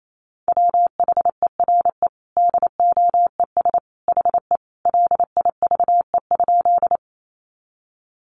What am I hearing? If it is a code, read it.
Morse code, heard as a keyed tone, about 28 wpm, 700 Hz. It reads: WHERE DOES HE LIVE?
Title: Sentences from Top 200 Words 28wpm